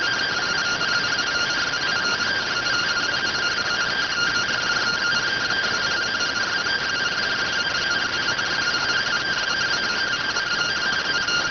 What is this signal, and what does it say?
Two-tone teleprinter signal, about 170 Hz shift.